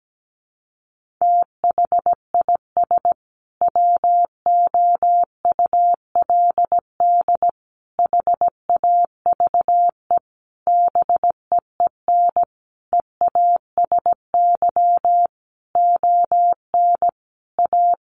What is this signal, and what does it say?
Morse code, heard as a keyed tone, about 17 wpm, 700 Hz. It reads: THIS WOULD HAVE BEEN EASY ON A
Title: Sentences from Top 800 Words 17wpm